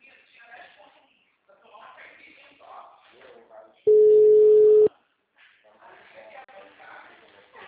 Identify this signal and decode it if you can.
Ringback tone (Europe, 425 Hz), with speech.